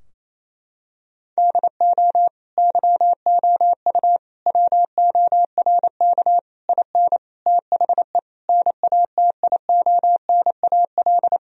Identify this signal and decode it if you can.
Morse code, heard as a keyed tone, about 28 wpm, 700 Hz. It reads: DO YOU WORK IN THE NATIONAL
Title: Sentences from Top 1000 Words - Review Part 4 - 28wpm